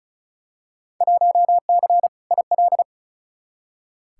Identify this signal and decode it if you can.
Morse code, heard as a keyed tone, about 36 wpm, 700 Hz. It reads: 1C IL
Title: Field Day Exchanges - 35wpm